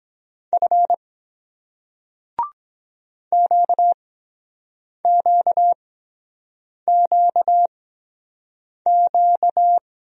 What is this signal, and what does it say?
Morse code, heard as a keyed tone, about 22 wpm, spaced as for 7 wpm, 700 Hz. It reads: F QQQQ